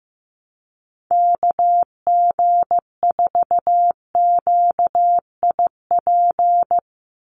Morse code, heard as a keyed tone, about 15 wpm, 700 Hz. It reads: KG4QIP
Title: USA Call Signs 15wpm